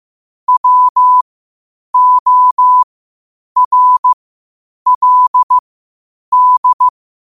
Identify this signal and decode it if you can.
Morse code, heard as a keyed tone, about 15 wpm, spaced as for 8 wpm, 1000 Hz. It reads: WORLD
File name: qotdfiles / 8wpm